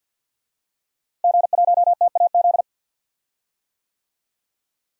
Morse code, heard as a keyed tone, about 48 wpm, 700 Hz. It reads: G1TAZ
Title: DXCC-100 Call Signs - 5 Characters - Repeated 2x - 50wpm